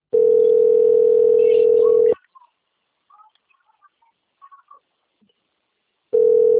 Ringback tone (North America), 440 + 480 Hz.